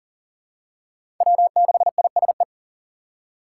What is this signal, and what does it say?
Morse code, heard as a keyed tone, about 40 wpm, 700 Hz. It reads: W6ISE
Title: USA Call Signs - 5 Characters - 6x Character Spacing - 40wpm